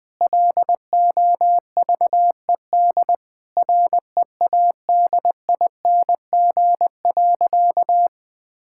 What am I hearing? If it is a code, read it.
Morse code, heard as a keyed tone, about 20 wpm, 700 Hz. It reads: LOVED READING.
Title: Sentences with 8-Word Phrases - Spoken Beginning - 20wpm